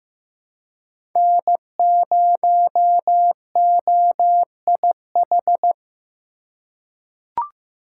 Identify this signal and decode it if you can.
Morse code, heard as a keyed tone, about 15 wpm, 700 Hz. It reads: N0OIH
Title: USA Call Signs - 5 Characters - 3x Character Spacing - 15wpm